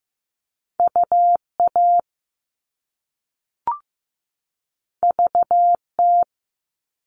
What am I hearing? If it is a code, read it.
Morse code, heard as a keyed tone, about 15 wpm, 700 Hz. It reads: UA VT
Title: All Cut Numbers - 2-Digit 15wpm